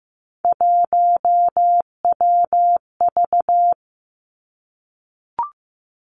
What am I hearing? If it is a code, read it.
Morse code, heard as a keyed tone, about 15 wpm, 700 Hz. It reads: 1WV
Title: Three Challenging Letter-Number Combinations 15wpm